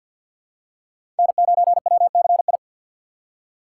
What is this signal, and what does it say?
Morse code, heard as a keyed tone, about 49 wpm, 700 Hz. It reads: N9WCI
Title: USA Call Signs - 2x Character Spacing - 50wpm